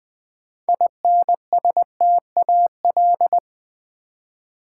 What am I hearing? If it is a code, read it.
Morse code, heard as a keyed tone, about 20 wpm, 700 Hz. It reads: INSTAL
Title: Build A Word - 6 Letters At A Time - 12-Letter Words - 20wpm